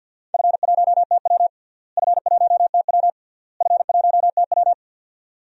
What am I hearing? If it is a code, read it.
Morse code, heard as a keyed tone, about 52 wpm, 700 Hz. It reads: F1TW F1TW F1TW
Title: DXCC-100 Call Signs - 4 Characters - Repeated 3x - 50wpm